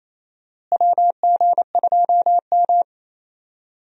Morse code, heard as a keyed tone, about 27 wpm, 700 Hz. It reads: WG2M
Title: USA Call Signs - 4 Characters - Repeated 2x - 28wpm